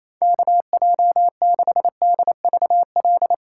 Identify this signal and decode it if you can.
Morse code, heard as a keyed tone, about 35 wpm, 700 Hz. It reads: KJ6DVL
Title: Speed Racing - USA Call Signs 28wpm